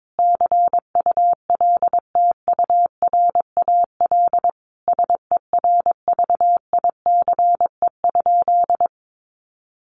Morse code, heard as a keyed tone, about 22 wpm, 700 Hz. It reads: CULTURAL SERVICE?
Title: Sentences from Top 1500 Words 22wpm